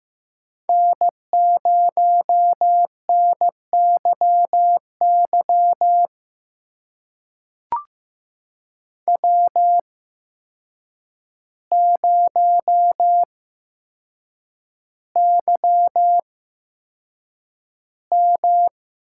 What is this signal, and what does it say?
Morse code, heard as a keyed tone, about 15 wpm, spaced as for 7 wpm, 700 Hz. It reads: N0NYY W0YM
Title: USA Call Signs - 8x Character Spacing - 15wpm